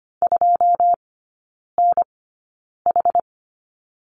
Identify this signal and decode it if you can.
Morse code, heard as a keyed tone, about 25 wpm, spaced as for 9 wpm, 700 Hz. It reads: 2NH
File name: Lesson 14e - Number 2 - Three Characters 25wpm